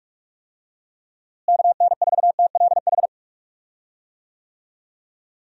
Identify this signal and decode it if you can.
Morse code, heard as a keyed tone, about 42 wpm, 700 Hz. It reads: KN4TLH
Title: USA Call Signs - 6 Characters - Repeated 3x - 45wpm